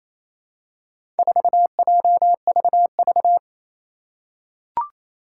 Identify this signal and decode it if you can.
Morse code, heard as a keyed tone, about 28 wpm, 700 Hz. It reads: 4JVV
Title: Four Challenging Letter-Number Combinations 28wpm